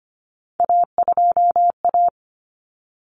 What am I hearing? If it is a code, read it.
Morse code, heard as a keyed tone, about 25 wpm, 700 Hz. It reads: A2A
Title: Common Cut Numbers - 3-Digit 25wpm